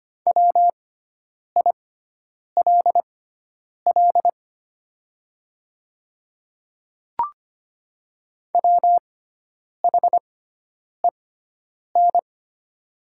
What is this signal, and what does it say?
Morse code, heard as a keyed tone, about 25 wpm, spaced as for 9 wpm, 700 Hz. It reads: WILL WHEN
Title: 6x Character Spacing - Top 100 Words - 25wpm